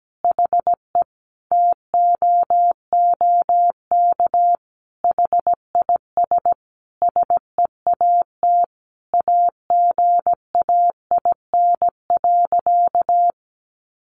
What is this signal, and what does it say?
Morse code, heard as a keyed tone, about 17 wpm, 700 Hz. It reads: HE TOOK HIS SEAT AGAIN.
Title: Sentences from Top 1500 Words 17wpm